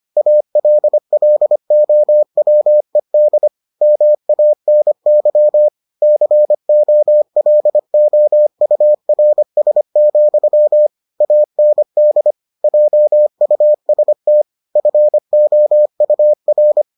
Morse code, heard as a keyed tone, about 25 wpm, 600 Hz. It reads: ALLOWED MANY COLOURS, AND JUST FOUR